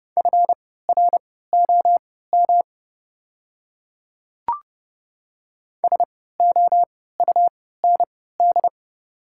Morse code, heard as a keyed tone, about 32 wpm, spaced as for 17 wpm, 700 Hz. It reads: FROM SOUND
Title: 3x Character Spacing - Top 100 Words - 30wpm